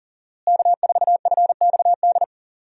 Morse code, heard as a keyed tone, about 40 wpm, 700 Hz. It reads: K4FXD
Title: USA Call Signs - 5 Characters - Repeated 3x - 40wpm